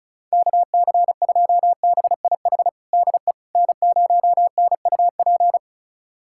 Morse code, heard as a keyed tone, about 36 wpm, 700 Hz. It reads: KC2BIH DE N0NUP
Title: USA Callsign DE Callsign 35wpm